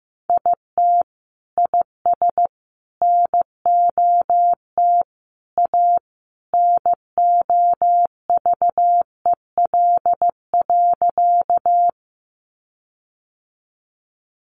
Morse code, heard as a keyed tone, about 15 wpm, 700 Hz. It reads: IT IS NOT A NOVEL.
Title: Sentences from Top 1300 Words 15wpm